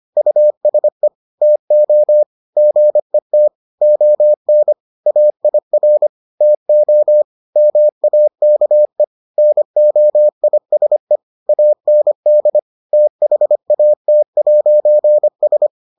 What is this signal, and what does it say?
Morse code, heard as a keyed tone, about 25 wpm, 600 Hz. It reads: USE TO GET ON AIR TO MAKE NOISE AND THAT'S